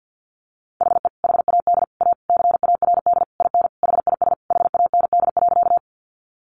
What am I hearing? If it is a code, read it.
Morse code, heard as a keyed tone, about 48 wpm, 700 Hz. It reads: HE HAD A CARD IN HIS HAND.
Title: Sentences from Top 600 Words 50wpm